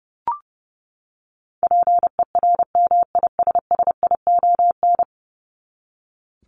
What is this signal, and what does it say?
Morse code, heard as a keyed tone, about 30 wpm, 700 Hz. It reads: PERMISSION
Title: Top 1501-1750 Words 30wpm